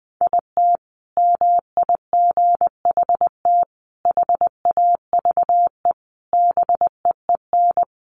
Morse code, heard as a keyed tone, about 20 wpm, 700 Hz. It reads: IT MIGHT HAVE BEEN
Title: Sentences from Top 1200 Words 20wpm